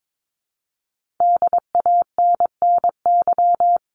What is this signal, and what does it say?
Morse code, heard as a keyed tone, about 22 wpm, 700 Hz. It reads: DANNY
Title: CWT Exchange 22wpm